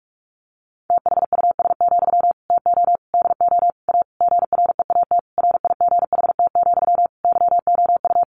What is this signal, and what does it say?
Morse code, heard as a keyed tone, about 44 wpm, 700 Hz. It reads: THUS, TO DO A GREAT RIGHT, YOU
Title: Sentences from Top 900 Words 45wpm